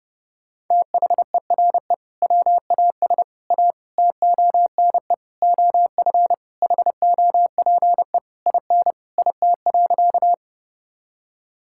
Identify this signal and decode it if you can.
Morse code, heard as a keyed tone, about 32 wpm, 700 Hz. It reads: THERE WAS A TONE OF HOPE IN IT.
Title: Sentences from Top 1600 Words 30wpm